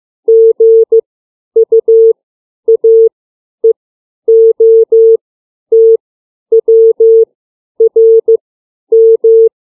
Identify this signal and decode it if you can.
Morse code, heard as a keyed tone, about 15 wpm, 450 Hz. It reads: G U A E O T W R M
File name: CW / alphabet 5